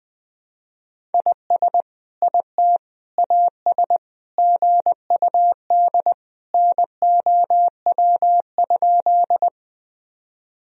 Morse code, heard as a keyed tone, about 21 wpm, 700 Hz. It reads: IS IT AS GUD NOW?
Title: Sentences from Top 200 Words and Abbreviations 20wpm